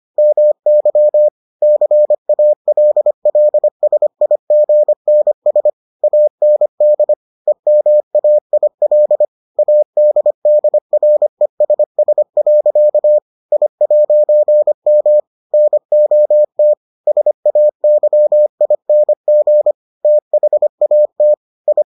Morse code, heard as a keyed tone, about 25 wpm, 600 Hz. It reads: MY CALLSIGNS AND EMAIL ADDRESS. I'M NOT SAYING THAT I